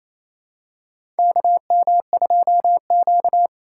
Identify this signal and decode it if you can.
Morse code, heard as a keyed tone, about 28 wpm, 700 Hz. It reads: KM2Q
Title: USA Call Signs - 4 Characters - 2x Character Spacing - 28wpm